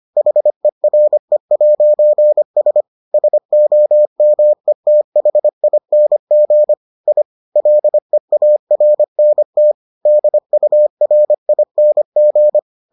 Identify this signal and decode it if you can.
Morse code, heard as a keyed tone, about 25 wpm, 600 Hz. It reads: HERE'S SOMETHING I LEARNT DURING